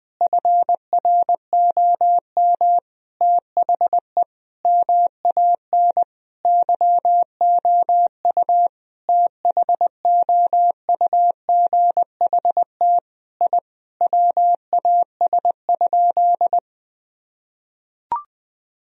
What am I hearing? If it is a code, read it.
Morse code, heard as a keyed tone, about 21 wpm, 700 Hz. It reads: FROM THE MAN YOU THOUGHT I WAS?
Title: Sentences from Top 1000 Words - Review Part 3 - 20wpm